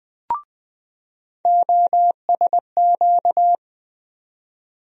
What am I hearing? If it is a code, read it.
Morse code, heard as a keyed tone, about 20 wpm, 700 Hz. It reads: OSQ
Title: USA Call Sign Postfixes - 3 Letters 20wpm